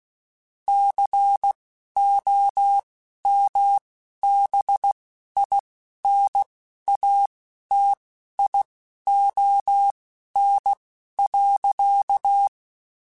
Morse code, heard as a keyed tone, about 16 wpm, spaced as for 11 wpm, 800 Hz. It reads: COMBINATION.
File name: Technical Subjects / Technical Subjects-mp3s / Sw Pwr Sup @10